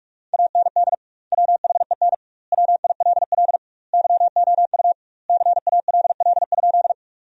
Morse code, heard as a keyed tone, about 46 wpm, 700 Hz. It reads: AND WHEN WILL YOU CALL?
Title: 2x Word-Spacing - Sentences from Top 100 Words - 45wpm